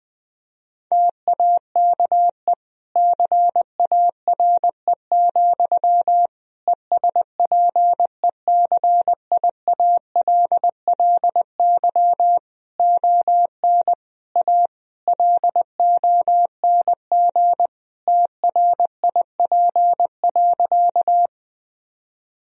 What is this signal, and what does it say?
Morse code, heard as a keyed tone, about 20 wpm, 700 Hz. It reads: TAKE CARE, ESPECIALLY ON A LONG TRIP.
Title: Sentences from Top 1000 Words - Review Part 4 - 20wpm